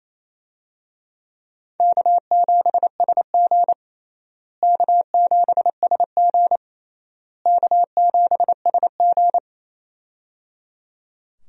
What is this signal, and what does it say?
Morse code, heard as a keyed tone, about 28 wpm, 700 Hz. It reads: K7SG K7SG K7SG
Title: USA Call Signs - 4 Characters - Repeated 3x - 28wpm